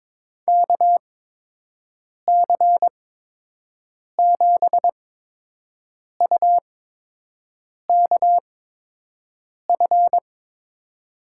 Morse code, heard as a keyed tone, about 22 wpm, spaced as for 6 wpm, 700 Hz. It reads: KC7UKF
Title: USA Call Signs - 6 Characters - 8x Character Spacing - 22wpm